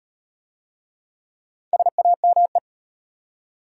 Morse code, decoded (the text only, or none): SAME